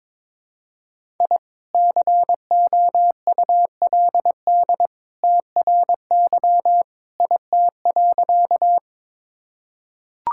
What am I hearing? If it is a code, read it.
Morse code, heard as a keyed tone, about 22 wpm, 700 Hz. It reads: I COULD TRY IT.
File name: practice / 22wpm / Sentences with 2-Word Phrases - 22wpm